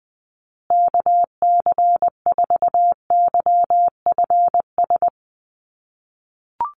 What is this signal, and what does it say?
Morse code, heard as a keyed tone, about 20 wpm, 700 Hz. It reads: KC4YFS
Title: USA Call Signs Repeated 3x 20wpm